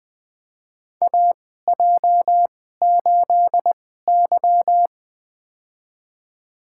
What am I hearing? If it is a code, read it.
Morse code, heard as a keyed tone, about 20 wpm, spaced as for 14 wpm, 700 Hz. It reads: AJ8Y
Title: USA Call Signs - 4 Characters - 2x Character Spacing - 20wpm